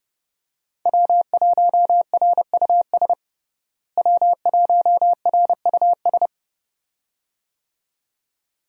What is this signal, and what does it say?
Morse code, heard as a keyed tone, about 32 wpm, 700 Hz. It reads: W1RUS W1RUS
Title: USA Call Signs - 5 Characters - Repeated 2x - 30wpm